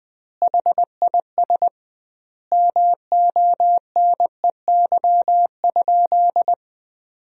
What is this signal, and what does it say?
Morse code, heard as a keyed tone, about 20 wpm, 700 Hz. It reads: HIS MONEY?
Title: 2x Word-Spacing - Sentences from Top 400 Words - 20wpm